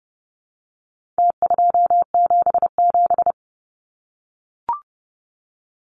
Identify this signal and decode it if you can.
Morse code, heard as a keyed tone, about 30 wpm, 700 Hz. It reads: T277
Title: Numbers - Contest Style - 4-Digit 30wpm